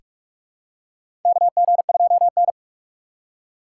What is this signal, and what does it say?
Morse code, heard as a keyed tone, about 43 wpm, 700 Hz. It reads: KG2N
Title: USA Call Signs - 4 Characters - 2x Character Spacing - 45wpm